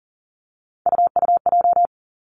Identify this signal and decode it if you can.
Morse code, heard as a keyed tone, about 41 wpm, 690 Hz. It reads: UUJ
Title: Three Challenging Letter-Number Combinations 40wpm